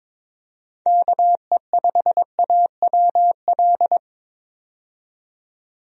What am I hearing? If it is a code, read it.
Morse code, heard as a keyed tone, about 22 wpm, 700 Hz. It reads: KE5AWL
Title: USA Call Signs - 6 Characters - 22wpm